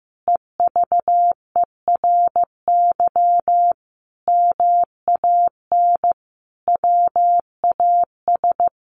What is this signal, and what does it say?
Morse code, heard as a keyed tone, about 15 wpm, 700 Hz. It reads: EVERY MAN WAS
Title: Sentences from Top 1500 Words 15wpm